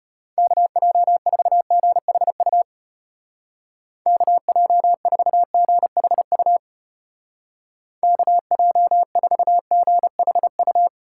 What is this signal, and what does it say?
Morse code, heard as a keyed tone, about 32 wpm, 700 Hz. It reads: KJ4GHU KJ4GHU KJ4GHU